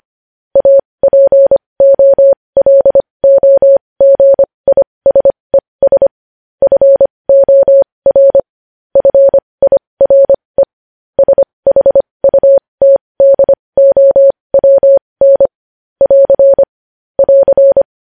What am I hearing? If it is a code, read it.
Morse code, heard as a keyed tone, about 25 wpm, spaced as for 20 wpm, 550 Hz. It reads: APOLOGISES FOR FIRE SHUTDOWN <AR> <AR>
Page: News Headlines in Morse Code at 20 WPM